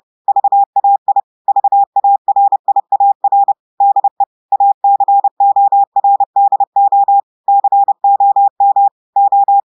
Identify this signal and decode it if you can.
Morse code, heard as a keyed tone, about 30 wpm, 800 Hz. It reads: VAI VARIAR DE ACORDO COM O
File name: portugais texte en clair 2 vitesse 30 mots minutes